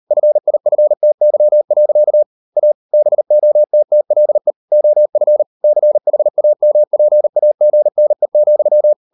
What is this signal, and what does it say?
Morse code, heard as a keyed tone, about 38 wpm, 600 Hz. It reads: FIFTY. A BOTTLE OF CHAMPAGNE,